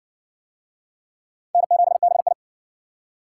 Morse code, heard as a keyed tone, about 55 wpm, 700 Hz. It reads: N7BI